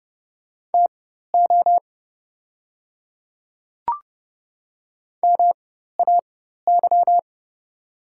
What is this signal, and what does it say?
Morse code, heard as a keyed tone, about 29 wpm, spaced as for 14 wpm, 700 Hz. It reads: TO MAY
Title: 4x Character Spacing - Top 100 Words - 30wpm